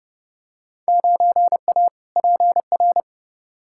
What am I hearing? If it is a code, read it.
Morse code, heard as a keyed tone, about 30 wpm, 700 Hz. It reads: 9A PR
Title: Field Day Exchanges Encore - 30wpm